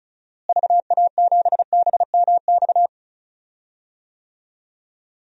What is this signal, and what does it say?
Morse code, heard as a keyed tone, about 34 wpm, 700 Hz. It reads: VA7BMX
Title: DXCC-100 Call Signs - 6 Characters - Repeated 3x - 35wpm